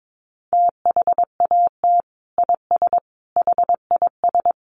Morse code, heard as a keyed tone, about 22 wpm, 700 Hz. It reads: THAT IS HIS
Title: Sentences from Top 900 Words 22wpm